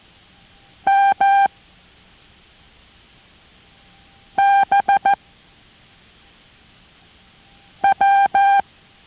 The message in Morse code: MBW